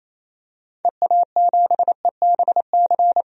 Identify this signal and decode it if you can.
Morse code, heard as a keyed tone, about 28 wpm, 700 Hz. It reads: EA7EBC
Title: DXCC-100 Call Signs - 6 Characters - 28wpm